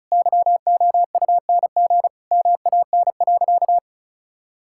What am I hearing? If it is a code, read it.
Morse code, heard as a keyed tone, about 34 wpm, 700 Hz. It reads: YOUNG MAN.
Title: Sentences from Top 1000 Words - Review Part 5 - 35wpm